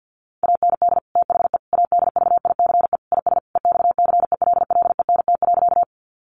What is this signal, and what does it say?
Morse code, heard as a keyed tone, about 50 wpm, 700 Hz. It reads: AND THE ADVICE IS EXCELLENT.
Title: Sentences from Top 1100 Words 50wpm